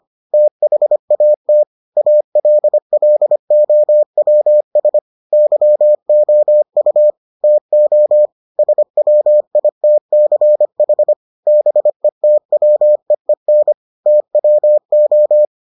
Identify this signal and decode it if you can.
Morse code, heard as a keyed tone, about 25 wpm, 600 Hz. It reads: THAT ALLOWS YOU TO SWITCH BETWEEN TWO